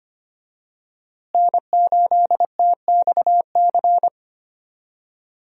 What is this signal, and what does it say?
Morse code, heard as a keyed tone, about 25 wpm, 700 Hz. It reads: N8TXC
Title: USA Call Signs - 5 Characters - Repeated 2x - 25wpm